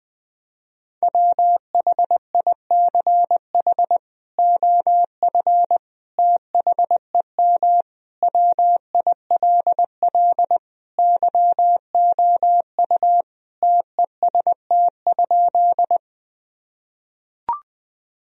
Morse code, heard as a keyed tone, about 21 wpm, 700 Hz. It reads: WHICH OF THEM WILL YOU TEST?
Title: Sentences from Top 1000 Words - Review Part 3 - 20wpm